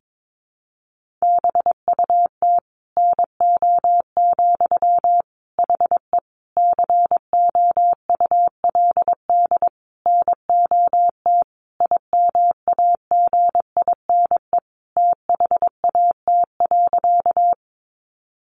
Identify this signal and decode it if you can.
Morse code, heard as a keyed tone, about 22 wpm, 700 Hz. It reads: BUT NO, HE COULD NOT IMAGINE THAT.
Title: Sentences from Top 1400 Words 22wpm